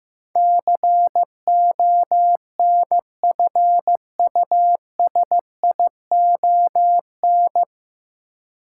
Morse code, heard as a keyed tone, about 15 wpm, 700 Hz. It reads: CONFUSION
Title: Sentences from Top 1600 Words - New Word Drill - 15wpm